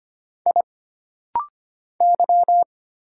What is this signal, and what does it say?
Morse code, heard as a keyed tone, about 25 wpm, 700 Hz. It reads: I Y
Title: Single Letter-Number - Rapid-Fire 25wpm